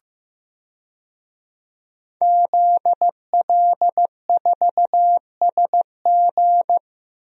Morse code, heard as a keyed tone, about 15 wpm, 700 Hz. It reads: ZL4SG
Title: DXCC-100 Call Signs - 5 Characters - Repeated 3x - 15wpm